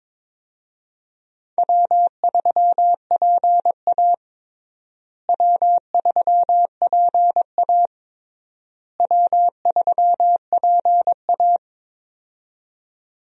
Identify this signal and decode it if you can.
Morse code, heard as a keyed tone, about 22 wpm, 700 Hz. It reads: W3PA W3PA W3PA
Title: USA Call Signs - 4 Characters - Repeated 3x - 22wpm